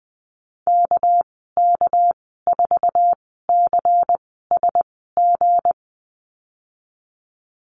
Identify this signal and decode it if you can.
Morse code, heard as a keyed tone, about 20 wpm, spaced as for 15 wpm, 700 Hz. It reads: KK4CSG
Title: 200 Lesson 32g - Letter k - Call Signs 20wpm